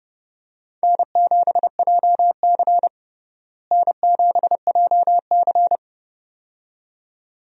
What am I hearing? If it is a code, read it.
Morse code, heard as a keyed tone, about 30 wpm, 700 Hz. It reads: N7JC N7JC
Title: USA Call Signs - 4 Characters - Repeated 2x - 30wpm